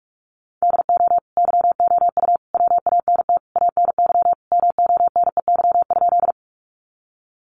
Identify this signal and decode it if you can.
Morse code, heard as a keyed tone, about 45 wpm, 700 Hz. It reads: DO YOU WANT ANY MONEY?
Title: Sentences from Top 400 Words 45wpm